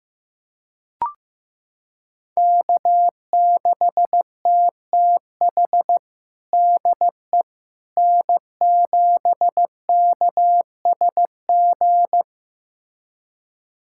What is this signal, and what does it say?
Morse code, heard as a keyed tone, about 16 wpm, 700 Hz. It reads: K6TTH DE N7KSG
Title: USA Callsign DE Callsign - 5 Characters - 15wpm